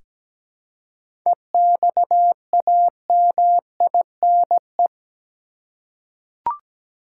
Morse code, heard as a keyed tone, about 17 wpm, 700 Hz. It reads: EXAMINE
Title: Seven Letter Words - 17wpm